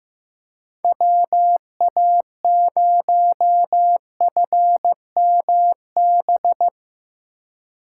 Morse code, heard as a keyed tone, about 17 wpm, 700 Hz. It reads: WA0FMB
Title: Speed Racing - USA 2x3 Call Signs 15wpm